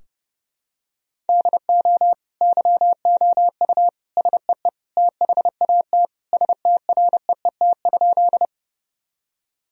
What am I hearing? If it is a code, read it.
Morse code, heard as a keyed tone, about 30 wpm, 700 Hz. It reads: DO YOU SEE THAT STREET?
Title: Sentences from Top 1000 Words - Review Part 3 - 30wpm